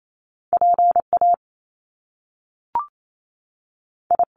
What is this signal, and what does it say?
Morse code, heard as a keyed tone, about 28 wpm, 700 Hz. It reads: PA I
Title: Top Two Letter Combinations in Words 28wpm